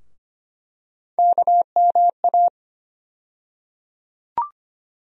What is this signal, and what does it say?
Morse code, heard as a keyed tone, about 25 wpm, 700 Hz. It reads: KMA
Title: USA Call Sign Postfixes - 3 Letters 25wpm